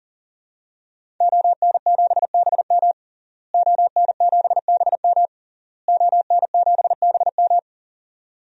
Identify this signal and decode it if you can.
Morse code, heard as a keyed tone, about 38 wpm, 700 Hz. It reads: ON7BM ON7BM ON7BM
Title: DXCC-100 Call Signs - 5 Characters - Repeated 3x - 40wpm